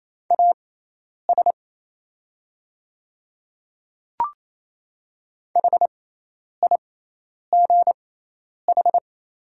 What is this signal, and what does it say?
Morse code, heard as a keyed tone, about 28 wpm, spaced as for 10 wpm, 700 Hz. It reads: AS HIGH
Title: 6x Character Spacing - Top 200 Words - 28wpm